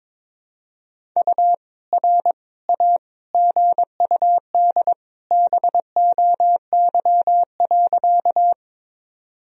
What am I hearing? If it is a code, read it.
Morse code, heard as a keyed tone, about 22 wpm, 700 Hz. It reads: U R A GUD BOY.
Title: Sentences from Top 200 Words and Abbreviations 22wpm